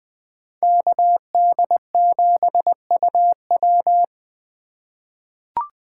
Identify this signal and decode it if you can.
Morse code, heard as a keyed tone, about 20 wpm, 700 Hz. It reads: KD7UW
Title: USA Call Signs - 8x Character Spacing - 20wpm